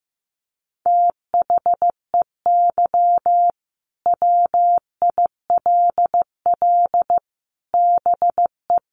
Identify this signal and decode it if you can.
Morse code, heard as a keyed tone, about 15 wpm, 700 Hz. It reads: THEY WILL BE
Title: Sets of 5 Words Encore 15wpm